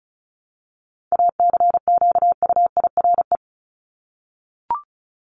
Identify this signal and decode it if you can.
Morse code, heard as a keyed tone, about 34 wpm, 700 Hz. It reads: ACQUIRE
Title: Top 1501-1750 Words 35wpm